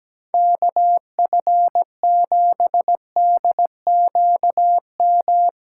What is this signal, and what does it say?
Morse code, heard as a keyed tone, about 17 wpm, 700 Hz. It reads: KF7DQM
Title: USA Call Signs - 6 Characters - 4x Character Spacing - 17wpm